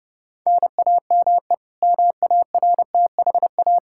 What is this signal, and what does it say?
Morse code, heard as a keyed tone, about 30 wpm, 700 Hz. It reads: NAME MARTHA
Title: Practice QSO - Exchange 1 - Beginning - 30wpm